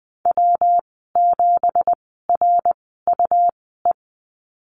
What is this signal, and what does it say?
Morse code, heard as a keyed tone, about 20 wpm, spaced as for 14 wpm, 700 Hz. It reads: W7RUE
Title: 173 Lesson 28g - Number 9 - Call Signs 20wpm